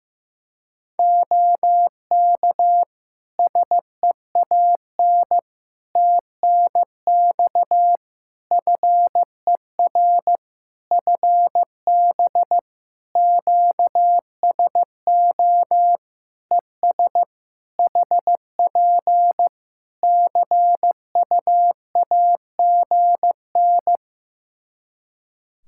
Morse code, heard as a keyed tone, about 15 wpm, 700 Hz. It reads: OK SEAN TNX FER FB QSO ES HP CUAGN
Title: Practice QSO - Exchange 4 - Ending - 15wpm